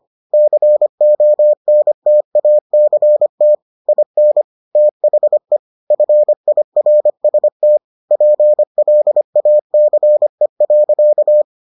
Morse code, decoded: CONTACT IN THE FIRST PLACE.

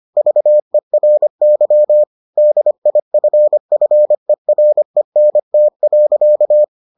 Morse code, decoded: VERY DIFFERENT.

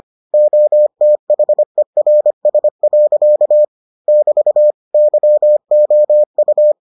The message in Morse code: OTHERS. <BT> YOU

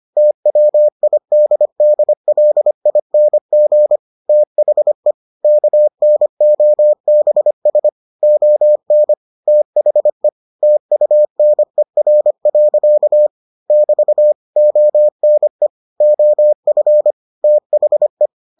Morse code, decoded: TWIDDLING THE KNOBS ON THE TUNER. <BT> ONE OF THE

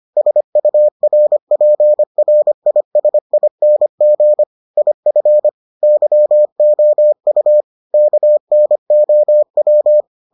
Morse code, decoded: SURPRISING IF YOU KNOW

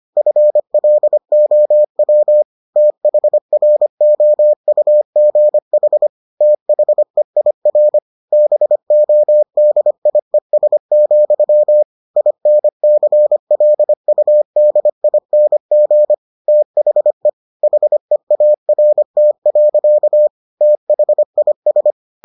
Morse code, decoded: FLOW THROUGH THEIR BODIES, INCLUDING THE HEART. THIS

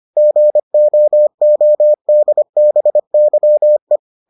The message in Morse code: GOODBYE

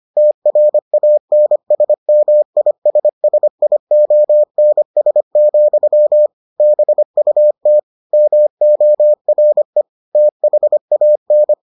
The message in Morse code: TRANSMISSIONS, BUT MORE THAN

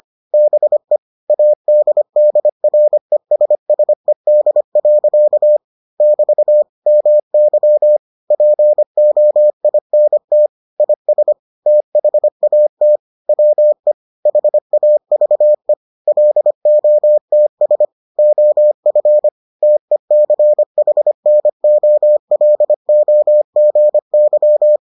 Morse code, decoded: BE ADDRESSED. <BT> MY POINT IS THAT WE HAVE LOTS OF TECHNOLOGY

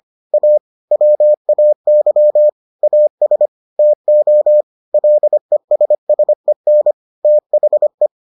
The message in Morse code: A WAY AS TO LESSEN THE

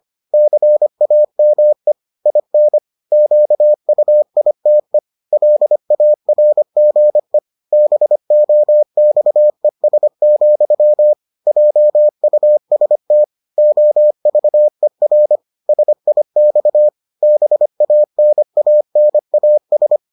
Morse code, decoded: CAME IN QUITE LARGE BOXES, JUST OVER SIX BANANAS

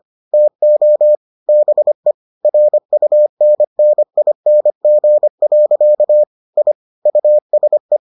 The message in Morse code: TO BE RUNNING. I USE

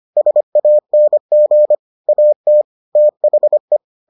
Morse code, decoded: SANG AT THE

